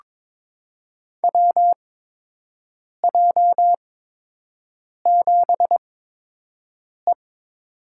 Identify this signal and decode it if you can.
Morse code, heard as a keyed tone, about 21 wpm, spaced as for 6 wpm, 700 Hz. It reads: WJ7E
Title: USA Call Signs - 4 Characters - 8x Character Spacing - 22wpm